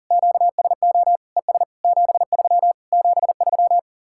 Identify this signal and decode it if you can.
Morse code, heard as a keyed tone, about 49 wpm, 700 Hz. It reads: QSO ES 73 73
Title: Speed-Racing Common QSO Phrases - 40wpm